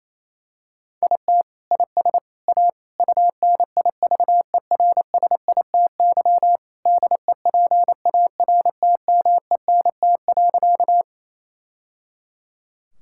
Morse code, heard as a keyed tone, about 28 wpm, 700 Hz. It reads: IT IS A UNIVERSITY DEPARTMENT.
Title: Sentences from Top 1000 Words - Review Part 5 - 28wpm